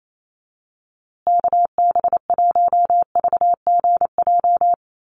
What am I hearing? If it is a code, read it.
Morse code, heard as a keyed tone, about 28 wpm, 700 Hz. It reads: KB1VGJ
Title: USA Call Signs Repeated 2x 28wpm